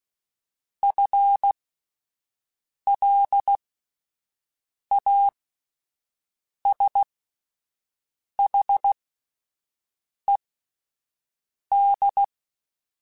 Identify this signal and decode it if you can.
Morse code, heard as a keyed tone, about 16 wpm, spaced as for 5 wpm, 800 Hz. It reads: FLASHED
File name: Spaced / FL Runs sp mp3s / FL Runs sp @10